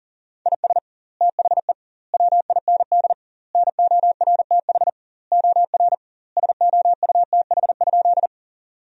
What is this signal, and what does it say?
Morse code, heard as a keyed tone, about 40 wpm, 700 Hz. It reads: IS THE WIND NORTH OR SOUTH?
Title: 2x Word-Spacing - Sentences from Top 400 Words - 40wpm